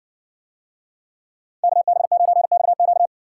Morse code, heard as a keyed tone, about 57 wpm, 700 Hz. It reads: KB9XQ